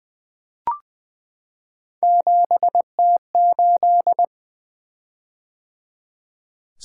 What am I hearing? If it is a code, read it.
Morse code, heard as a keyed tone, about 20 wpm, 700 Hz. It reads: 7T8
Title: Three Letter-Number Combinations - 20wpm